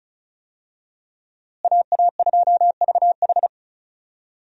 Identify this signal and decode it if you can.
Morse code, heard as a keyed tone, about 36 wpm, 700 Hz. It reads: AA2VH
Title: USA Call Signs - 5 Characters - Repeated 3x - 35wpm